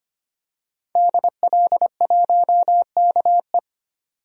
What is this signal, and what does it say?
Morse code, heard as a keyed tone, about 25 wpm, 700 Hz. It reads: DL1KE
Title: DXCC-100 Call Signs - 5 Characters - 25wpm